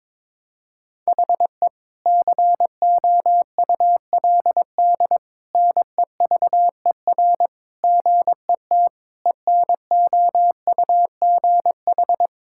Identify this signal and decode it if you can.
Morse code, heard as a keyed tone, about 22 wpm, 700 Hz. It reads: HE COULD NEVER GET ENOUGH
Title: Sentences from Top 1000 Words - Review Part 5 - 22wpm